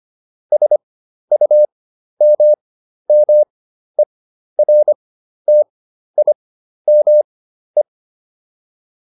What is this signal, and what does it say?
Morse code, heard as a keyed tone, about 25 wpm, spaced as for 12 wpm, 600 Hz. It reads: SUMMERTIME